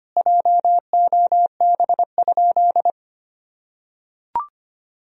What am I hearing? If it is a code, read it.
Morse code, heard as a keyed tone, about 25 wpm, 700 Hz. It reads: JOB?
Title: Sentences from Top 1000 Words - Review Part 3 - 25wpm